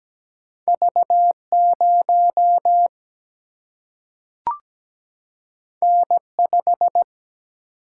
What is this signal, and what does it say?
Morse code, heard as a keyed tone, about 17 wpm, 700 Hz. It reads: V 0 N 5
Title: Single-Letter Single-Number 17wpm